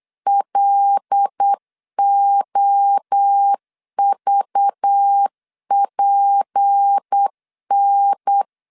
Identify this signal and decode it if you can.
Morse code, heard as a keyed tone, about 8 wpm, 800 Hz. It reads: LOVPN
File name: LEZIONI / 6 PKX0 / 6 PKX0 8wpm 3341